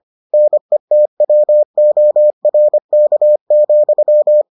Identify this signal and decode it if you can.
Morse code, heard as a keyed tone, about 25 wpm, 600 Hz. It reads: NETWORK,